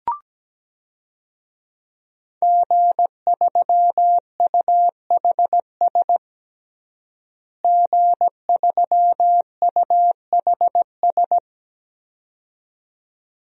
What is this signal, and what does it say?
Morse code, heard as a keyed tone, about 17 wpm, 700 Hz. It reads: G3UHS G3UHS
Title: DXCC-100 Call Signs - 5 Characters - Repeated 2x - 17wpm